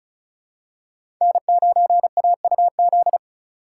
Morse code, heard as a keyed tone, about 36 wpm, 700 Hz. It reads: N9AUZ
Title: USA Call Signs - 2x Character Spacing - 35wpm